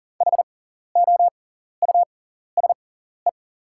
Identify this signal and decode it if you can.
Morse code, heard as a keyed tone, about 40 wpm, spaced as for 14 wpm, 700 Hz. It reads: HOUSE
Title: 6x Character Spacing - Top 300 Words - 40wpm